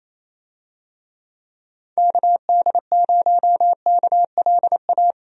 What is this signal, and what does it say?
Morse code, heard as a keyed tone, about 27 wpm, 700 Hz. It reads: KD0KLA
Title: USA Call Signs - 6 Characters - Repeated 3x - 28wpm